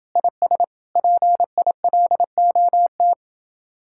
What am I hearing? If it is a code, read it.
Morse code, heard as a keyed tone, about 30 wpm, 700 Hz. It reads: IS PILOT